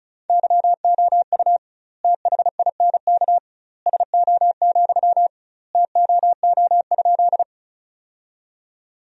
Morse code, heard as a keyed tone, about 36 wpm, 700 Hz. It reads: YOU THINK SO, TOO?
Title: 2x Word-Spacing - Sentences from Top 200 Words - 35wpm